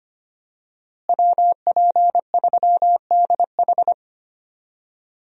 Morse code, heard as a keyed tone, about 25 wpm, 700 Hz. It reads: WP3DH
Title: USA Call Signs - 5 Characters - Repeated 2x - 25wpm